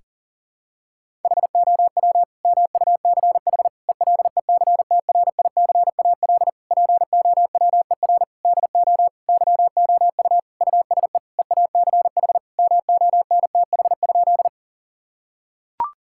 Morse code, heard as a keyed tone, about 40 wpm, 700 Hz. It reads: HOW MUCH ELECTRICAL POWER DO YOU USE EACH MONTH?
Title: Sentences from Top 1000 Words - Review Part 6 - 40wpm